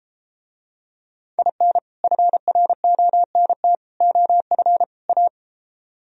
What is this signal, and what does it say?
Morse code, heard as a keyed tone, about 33 wpm, 700 Hz. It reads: IN FRONT OF A